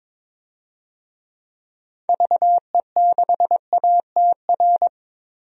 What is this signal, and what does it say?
Morse code, heard as a keyed tone, about 22 wpm, 700 Hz. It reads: VE6ATR
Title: DXCC-100 Call Signs - 6 Characters - 22wpm